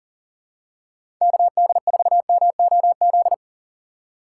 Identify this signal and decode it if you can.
Morse code, heard as a keyed tone, about 40 wpm, 700 Hz. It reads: KD4MOZ
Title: USA Call Signs - 6 Characters - 3x Character Spacing - 40wpm